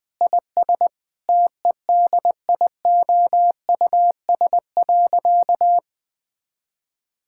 Morse code, heard as a keyed tone, about 20 wpm, 700 Hz. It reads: IS TEDIOUS.
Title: Sentences with 7-Word Phrases - Spoken Beginning - 20wpm